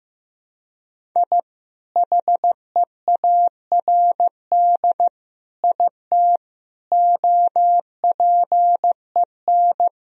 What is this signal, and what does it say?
Morse code, heard as a keyed tone, about 15 wpm, 700 Hz. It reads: I HEARD IT OPEN
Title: Sentences from Top 1000 Words - Review Part 1 - 15wpm